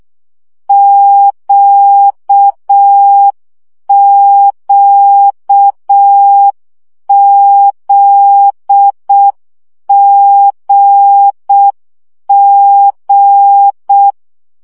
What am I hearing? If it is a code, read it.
Morse code, heard as a keyed tone, about 6 wpm, 800 Hz. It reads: QQZGG
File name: CW gyakorlas g z q 5 baud